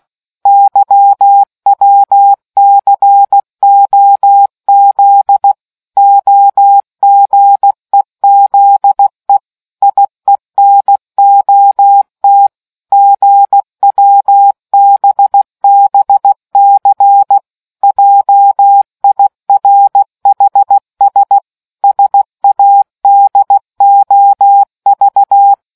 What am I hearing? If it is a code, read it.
Morse code, heard as a keyed tone, about 16 wpm, 800 Hz. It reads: YWCOZ OGEZE IENOT GWBBC JIRHS SADOV